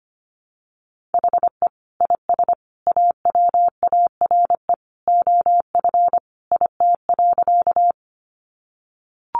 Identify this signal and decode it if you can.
Morse code, heard as a keyed tone, about 25 wpm, 700 Hz. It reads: HE IS AWARE OF IT.
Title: Sentences from Top 1200 Words 25wpm